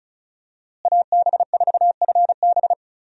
Morse code, heard as a keyed tone, about 36 wpm, 700 Hz. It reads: AB4FB
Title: USA Call Signs - 5 Characters - Repeated 3x - 35wpm